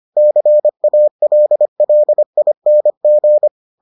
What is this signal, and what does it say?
Morse code, heard as a keyed tone, about 25 wpm, 600 Hz. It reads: CALLING